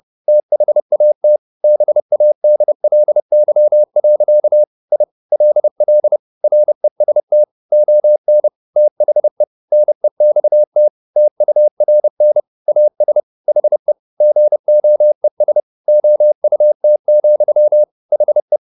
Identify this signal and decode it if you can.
Morse code, heard as a keyed tone, about 30 wpm, 600 Hz. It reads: THAT BADLY. I LL REST ON THE NEXT TURN AS HE GOES OUT, HE